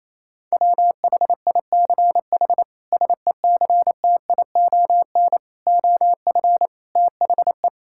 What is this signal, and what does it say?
Morse code, heard as a keyed tone, about 28 wpm, 700 Hz. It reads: WHICH SECTION OF THE
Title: Sentences from Top 1000 Words - Review Part 5 - 28wpm